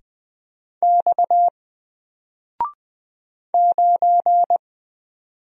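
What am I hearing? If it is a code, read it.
Morse code, heard as a keyed tone, about 20 wpm, 700 Hz. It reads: X 9
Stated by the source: Single letter and number Morse drill at 20 wpm, from N4LCD.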